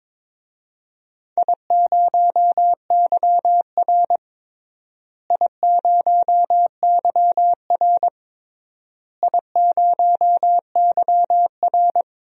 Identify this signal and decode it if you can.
Morse code, heard as a keyed tone, about 22 wpm, 700 Hz. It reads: I0YR I0YR I0YR
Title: DXCC-100 Call Signs - 4 Characters - Repeated 3x - 22wpm